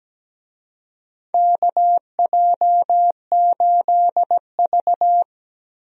Morse code, heard as a keyed tone, about 17 wpm, 700 Hz. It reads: KJ8V
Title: USA Call Signs - 4 Characters - Repeated 3x - 17wpm